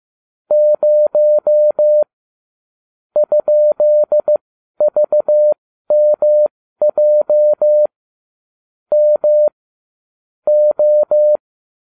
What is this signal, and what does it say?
Morse code, heard as a keyed tone, about 15 wpm, spaced as for 12 wpm, 600 Hz. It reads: 0 ?VMJ M O